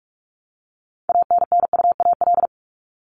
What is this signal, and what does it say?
Morse code, heard as a keyed tone, about 44 wpm, 700 Hz. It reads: ANNUAL
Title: Top 1251-1500 Words 45wpm